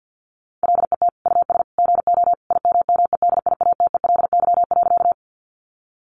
Morse code, heard as a keyed tone, about 50 wpm, 700 Hz. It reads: LET US GO IMMEDIATELY.
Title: Sentences from Top 1100 Words 50wpm